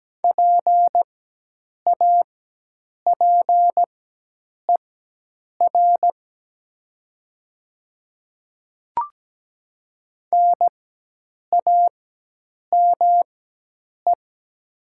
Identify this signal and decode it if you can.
Morse code, heard as a keyed tone, about 17 wpm, spaced as for 8 wpm, 700 Hz. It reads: PAPER NAME
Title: 4x Character Spacing - Top 300 Words - 17wpm